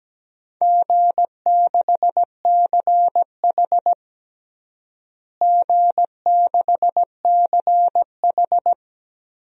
Morse code, heard as a keyed tone, about 17 wpm, 700 Hz. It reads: G6CH G6CH
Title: DXCC-100 Call Signs - 4 Characters - Repeated 2x - 17wpm